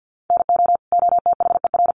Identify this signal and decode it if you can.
Morse code, heard as a keyed tone, about 50 wpm, 700 Hz. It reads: NO OTHER
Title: Sentences from Top 1500 Words 50wpm